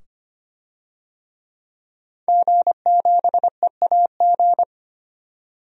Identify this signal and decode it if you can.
Morse code, heard as a keyed tone, about 25 wpm, 700 Hz. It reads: G7EAG
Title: DXCC-100 Call Signs - 5 Characters - 25wpm